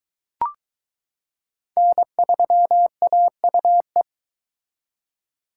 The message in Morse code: N3AUE

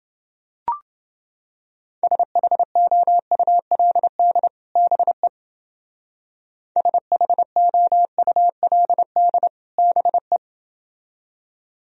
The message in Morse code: SHOULD BE SHOULD BE

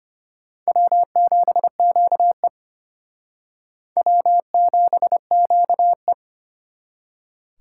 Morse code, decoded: W7QE W7QE

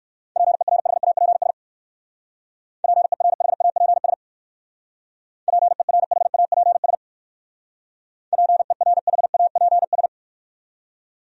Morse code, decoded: PERHAPS PERHAPS PERHAPS PERHAPS

